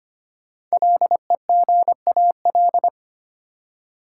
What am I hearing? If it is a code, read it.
Morse code, heard as a keyed tone, about 25 wpm, 700 Hz. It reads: LEGAL
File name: 5 Letter Words - First 3 Letters Given - 25wpm